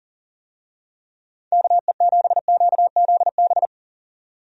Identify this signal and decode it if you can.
Morse code, heard as a keyed tone, about 40 wpm, 700 Hz. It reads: KE7QZB
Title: USA Call Signs - 6 Characters - 40wpm